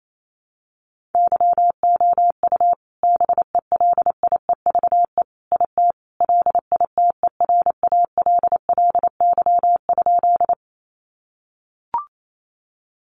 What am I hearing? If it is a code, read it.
Morse code, heard as a keyed tone, about 28 wpm, 700 Hz. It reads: YOU BELIEVE IT LITERALLY?
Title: Sentences from Top 1400 Words 28wpm